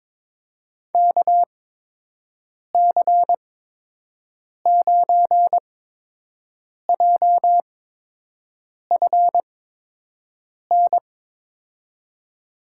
Morse code, heard as a keyed tone, about 22 wpm, spaced as for 6 wpm, 700 Hz. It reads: KC9JFN
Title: USA Call Signs - 6 Characters - 8x Character Spacing - 22wpm